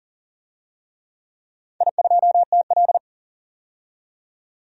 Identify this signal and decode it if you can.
Morse code, heard as a keyed tone, about 44 wpm, 690 Hz. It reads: I2TL
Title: DXCC-100 Call Signs - 4 Characters - 40wpm